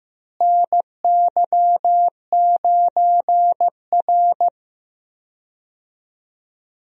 Morse code, heard as a keyed tone, about 16 wpm, 700 Hz. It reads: NY9R
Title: USA Call Signs - 4 Characters - Repeated 3x - 15wpm